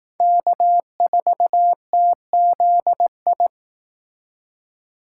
Morse code, heard as a keyed tone, about 18 wpm, 700 Hz. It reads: K4TZI
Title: Speed Racing - USA 1x3 Call Signs 15wpm